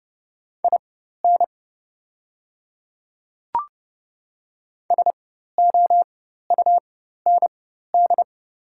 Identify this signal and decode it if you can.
Morse code, heard as a keyed tone, about 30 wpm, spaced as for 14 wpm, 700 Hz. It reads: IN SOUND
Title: 4x Character Spacing - Top 300 Words - 30wpm